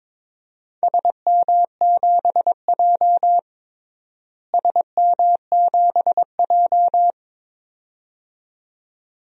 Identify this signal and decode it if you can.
Morse code, heard as a keyed tone, about 22 wpm, 700 Hz. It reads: SM7J SM7J
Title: DXCC-100 Call Signs - 4 Characters - Repeated 2x - 22wpm